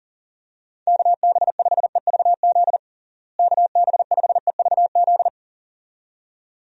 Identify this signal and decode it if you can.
Morse code, heard as a keyed tone, about 37 wpm, 700 Hz. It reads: KB5EVZ KB5EVZ
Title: USA Call Signs - 6 Characters - Repeated 2x - 40wpm